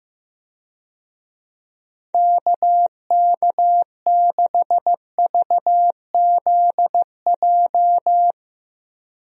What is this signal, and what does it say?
Morse code, heard as a keyed tone, about 15 wpm, 700 Hz. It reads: KK6VZJ
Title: USA Call Signs - 6 Characters - Repeated 2x - 15wpm